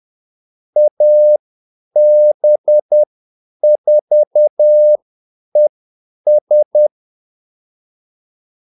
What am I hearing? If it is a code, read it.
Morse code, heard as a keyed tone, about 10 wpm, spaced as for 8 wpm, 600 Hz. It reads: AB4ES
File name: Callsigns-10-8wpm